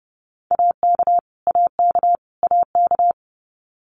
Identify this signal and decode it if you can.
Morse code, heard as a keyed tone, about 30 wpm, 700 Hz. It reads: AK AK AK
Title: US States Repeated 3x 30wpm